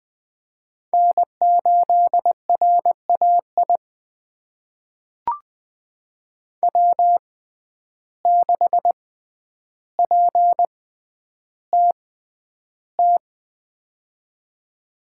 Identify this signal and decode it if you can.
Morse code, heard as a keyed tone, about 20 wpm, spaced as for 10 wpm, 700 Hz. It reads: N8RAI W6PTT
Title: USA Call Signs - 5 Characters - 6x Character Spacing - 20wpm